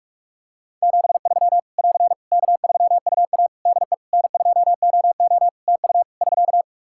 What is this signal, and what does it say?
Morse code, heard as a keyed tone, about 46 wpm, 700 Hz. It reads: 73 <AR> K3UA DE N2OO TU <SK>
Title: K1USN SST QSO Practice 45wpm